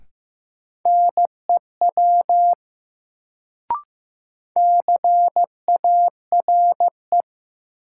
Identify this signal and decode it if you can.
Morse code, heard as a keyed tone, about 16 wpm, 700 Hz. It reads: NEW CARE
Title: Sentences from Top 300 Words - New Word Drill - 15wpm